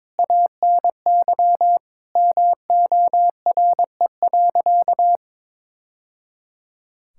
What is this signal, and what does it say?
Morse code, heard as a keyed tone, about 22 wpm, 700 Hz. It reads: ANY MORE.
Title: Sentences from Top 1000 Words - Review Part 5 - 22wpm